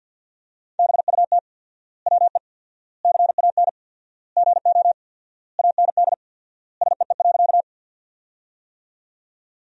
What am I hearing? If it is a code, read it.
Morse code, heard as a keyed tone, about 54 wpm, 700 Hz. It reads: BUT WE CAN GO AND SEE.
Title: 4x Word-Spacing - Sentences from Top 100 Words - 50wpm